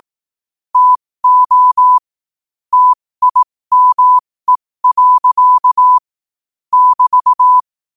Morse code, heard as a keyed tone, about 18 wpm, spaced as for 15 wpm, 1000 Hz. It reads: TO TIME. <BT>
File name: qotdfiles / 15wpm